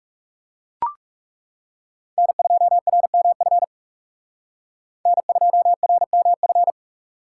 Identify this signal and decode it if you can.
Morse code, heard as a keyed tone, about 39 wpm, 700 Hz. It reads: N2RMF N2RMF